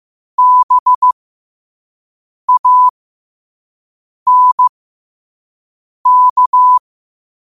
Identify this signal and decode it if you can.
Morse code, heard as a keyed tone, about 15 wpm, spaced as for 5 wpm, 1000 Hz. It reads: BANK
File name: qotdfiles / 5wpm